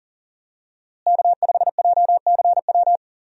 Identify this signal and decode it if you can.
Morse code, heard as a keyed tone, about 40 wpm, 700 Hz. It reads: K5JCW
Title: USA Call Signs - 5 Characters - Repeated 3x - 40wpm